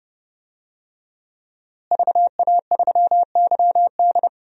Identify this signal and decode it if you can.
Morse code, heard as a keyed tone, about 31 wpm, 700 Hz. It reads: VA3YD
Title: DXCC-100 Call Signs - 5 Characters - Repeated 2x - 30wpm